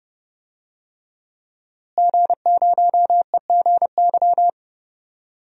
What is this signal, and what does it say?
Morse code, heard as a keyed tone, about 30 wpm, 700 Hz. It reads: G0EGY
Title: DXCC-100 Call Signs - 5 Characters - Repeated 3x - 30wpm